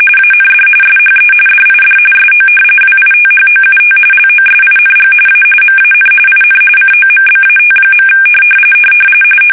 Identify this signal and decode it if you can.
RTTY, about 45.45 baud. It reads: AND SHE DID SO WITHOUT LOSING A MAN IN COMBAT.
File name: fsk850 / battleshipnj